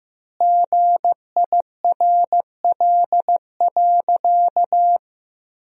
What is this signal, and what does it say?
Morse code, heard as a keyed tone, about 15 wpm, 700 Hz. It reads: GIRL.
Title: 2x Word-Spacing - Sentences from Top 400 Words - 15wpm